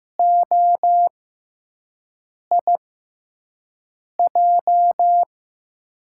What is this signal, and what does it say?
Morse code, heard as a keyed tone, about 15 wpm, spaced as for 5 wpm, 700 Hz. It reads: OIJ
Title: USA Call Signs - 6x Character Spacing - 15wpm